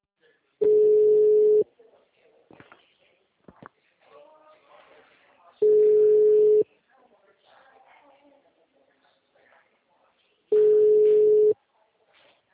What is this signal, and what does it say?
Ringback tone (Europe, 425 Hz), with speech.
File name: out- -2001- - - .4343